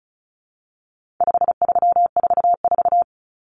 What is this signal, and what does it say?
Morse code, heard as a keyed tone, about 34 wpm, 700 Hz. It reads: 5344
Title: Numbers - 4-Digit 35wpm